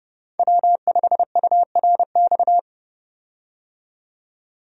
Morse code, heard as a keyed tone, about 30 wpm, 700 Hz. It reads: W5URX
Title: USA Call Signs - 5 Characters - 30wpm